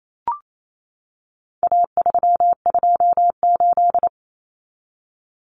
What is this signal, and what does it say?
Morse code, heard as a keyed tone, about 28 wpm, 700 Hz. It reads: A328
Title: Common Cut Numbers - 4-Digit 28wpm